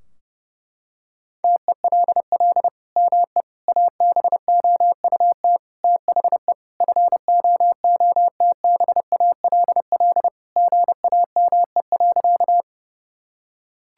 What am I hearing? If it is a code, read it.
Morse code, heard as a keyed tone, about 32 wpm, 700 Hz. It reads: TELL ME ABOUT THE FOOTBALL GAME.
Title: Sentences from Top 1600 Words 30wpm